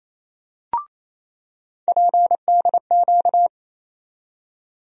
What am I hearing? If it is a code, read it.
Morse code, heard as a keyed tone, about 28 wpm, 700 Hz. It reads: PDQ
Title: USA Call Sign Postfixes - 3 Letters 28wpm